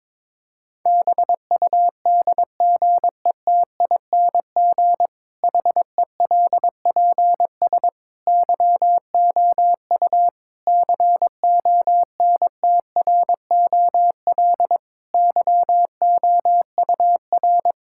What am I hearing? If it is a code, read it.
Morse code, heard as a keyed tone, about 22 wpm, 700 Hz. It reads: BUDGETING HELPS YOU CONTROL YOUR
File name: Difficult Sentences with Hints - 22wpm